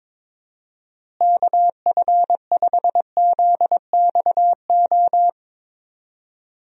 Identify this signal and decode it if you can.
Morse code, heard as a keyed tone, about 22 wpm, 700 Hz. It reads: KF5ZXO
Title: USA Call Signs - 6 Characters - Repeated 3x - 22wpm